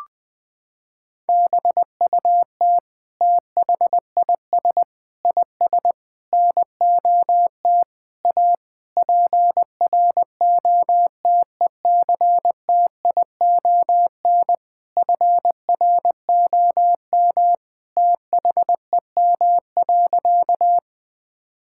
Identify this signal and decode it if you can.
Morse code, heard as a keyed tone, about 20 wpm, 700 Hz. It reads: BUT THIS IS NOT A PROTECTION FROM THEM.
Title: Sentences from Top 1600 Words 20wpm